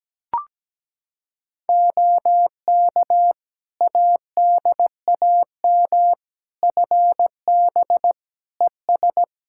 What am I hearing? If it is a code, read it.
Morse code, heard as a keyed tone, about 17 wpm, 700 Hz. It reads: OK ADAM FB ES
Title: Practice QSO - Exchange 2 - Info - 17wpm